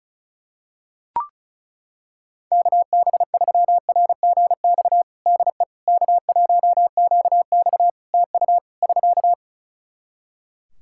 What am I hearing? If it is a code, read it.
Morse code, heard as a keyed tone, about 36 wpm, 700 Hz. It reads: KB3RGX DE K1QX TU <SK>
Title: Practice QSO - Exchange 4 - Ending - 35wpm